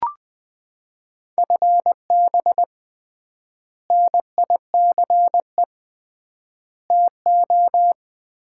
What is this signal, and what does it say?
Morse code, heard as a keyed tone, about 20 wpm, 700 Hz. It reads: FB NICE TO
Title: Common QSO Phrases - 3x Word-Spacing - 20wpm